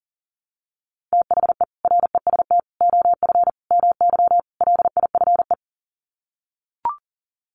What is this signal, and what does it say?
Morse code, heard as a keyed tone, about 40 wpm, 700 Hz. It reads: THE REST OF MY LIFE
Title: Sets of 5 Words 40wpm